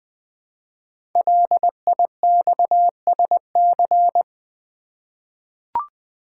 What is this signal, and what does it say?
Morse code, heard as a keyed tone, about 20 wpm, 700 Hz. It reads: LIXSC
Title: Code Groups - 5 Letters - 20wpm